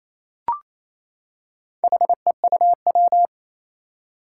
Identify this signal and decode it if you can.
Morse code, heard as a keyed tone, about 28 wpm, 700 Hz. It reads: HEUW
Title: Four Challenging Letter-Number Combinations 28wpm